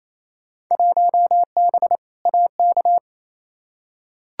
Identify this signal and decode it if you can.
Morse code, heard as a keyed tone, about 28 wpm, 700 Hz. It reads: 1B AK
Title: Field Day Exchanges Encore - 28wpm